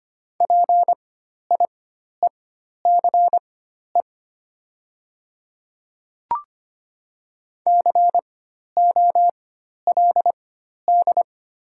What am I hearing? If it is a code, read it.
Morse code, heard as a keyed tone, about 25 wpm, spaced as for 12 wpm, 700 Hz. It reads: PIECE COLD
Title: 4x Character Spacing - Top 500 Words - 25wpm